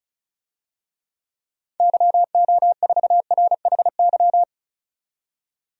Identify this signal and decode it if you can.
Morse code, heard as a keyed tone, about 34 wpm, 700 Hz. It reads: YO4RHY
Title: DXCC-100 Call Signs - 6 Characters - 35wpm